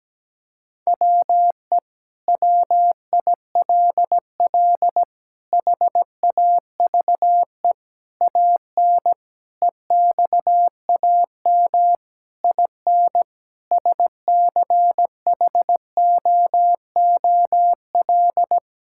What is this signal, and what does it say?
Morse code, heard as a keyed tone, about 17 wpm, 700 Hz. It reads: WE WILL HAVE AN EXAM IN SCHOOL
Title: Sentences from Top 1000 Words - Review Part 5 - 17wpm